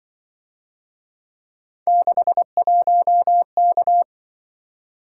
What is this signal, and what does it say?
Morse code, heard as a keyed tone, about 25 wpm, 700 Hz. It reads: 61K
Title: Speed-Racing - Three Letter-Number Combinations - 20wpm